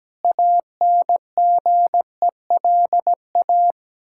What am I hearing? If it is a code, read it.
Morse code, heard as a keyed tone, about 17 wpm, 700 Hz. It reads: ANGELA
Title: Practice QSO - Exchange 1 - Beginning - 17wpm